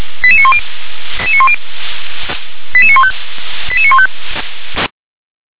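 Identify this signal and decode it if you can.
Five-tone selective call (ZVEI), heard as quick runs of five tones, tones of about 70 ms. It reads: 88910 88915 88915
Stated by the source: ZVEI five-tone call received by SDR.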